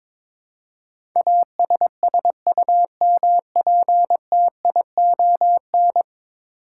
Morse code, heard as a keyed tone, about 22 wpm, 700 Hz. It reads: ASSUMPTION
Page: morse-code-ninja-practice-22wpm